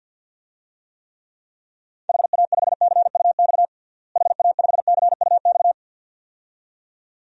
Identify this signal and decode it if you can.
Morse code, heard as a keyed tone, about 52 wpm, 700 Hz. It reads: HA5CUX HA5CUX
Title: DXCC-100 Call Signs - 6 Characters - Repeated 2x - 50wpm